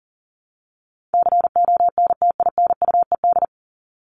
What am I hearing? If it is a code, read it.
Morse code, heard as a keyed tone, about 40 wpm, 700 Hz. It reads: CONTINUED
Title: ED Suffix - Top 100 Words with Base Words 40wpm